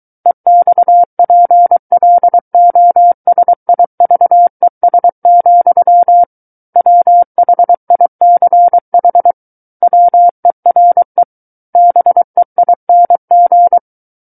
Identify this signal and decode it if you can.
Morse code, heard as a keyed tone, about 23 wpm, 700 Hz. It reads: EXPLOSIVES, WHICH WERE BEING
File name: Book One / 14-Chapter 14 - 23 WPM